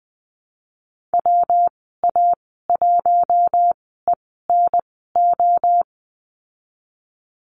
Morse code, heard as a keyed tone, about 20 wpm, spaced as for 15 wpm, 700 Hz. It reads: WA1ENO
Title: 179 Lesson 29g - Slash - Call Signs 20wpm